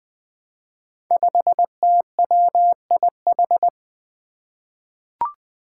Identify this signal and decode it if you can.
Morse code, heard as a keyed tone, about 20 wpm, 700 Hz. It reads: 5TWIH
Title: Five Challenging Letter-Number Combinations 20wpm